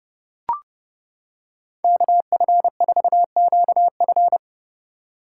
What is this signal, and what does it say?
Morse code, heard as a keyed tone, about 30 wpm, 700 Hz. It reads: KF4QF